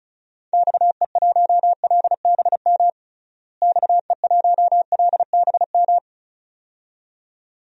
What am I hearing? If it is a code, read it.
Morse code, heard as a keyed tone, about 36 wpm, 700 Hz. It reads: XE1LBM XE1LBM
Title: DXCC-100 Call Signs - 6 Characters - Repeated 2x - 35wpm